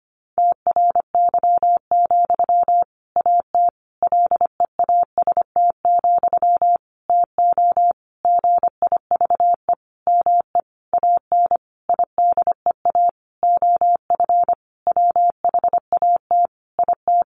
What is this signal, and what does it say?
Morse code, heard as a keyed tone, about 25 wpm, 700 Hz. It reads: TRY, AT LEAST, TO GIVE ME AN IDEA OF WHAT IT
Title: Sentences from Top 700 Words 25wpm